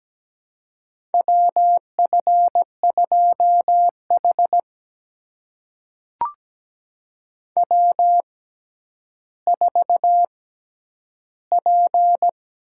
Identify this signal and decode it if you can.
Morse code, heard as a keyed tone, about 17 wpm, spaced as for 10 wpm, 700 Hz. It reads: WF2H W4P
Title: USA Call Signs - 4 Characters - 6x Character Spacing - 17wpm